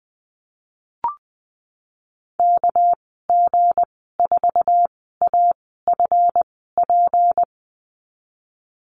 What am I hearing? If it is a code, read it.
Morse code, heard as a keyed tone, about 20 wpm, 700 Hz. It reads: K G 4 A F P
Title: 238 Lesson 38g - Letter q - Call Signs 20wpm